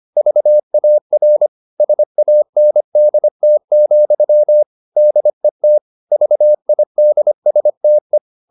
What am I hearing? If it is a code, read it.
Morse code, heard as a keyed tone, about 25 wpm, 600 Hz. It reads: VAR SANDT, DET VIDSTE
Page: Ferie 25 wpm